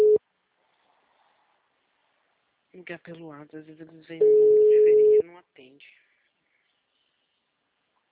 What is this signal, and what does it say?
Ringback tone (Europe, 425 Hz), with speech.
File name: out- -2017- - - .20701